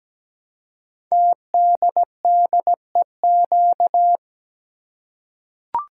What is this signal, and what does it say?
Morse code, heard as a keyed tone, about 17 wpm, 700 Hz. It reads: TDDEQ
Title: Code Groups - 5 Letters - 17wpm